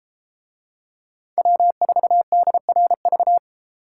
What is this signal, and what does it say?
Morse code, heard as a keyed tone, about 32 wpm, 700 Hz. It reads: W4DRV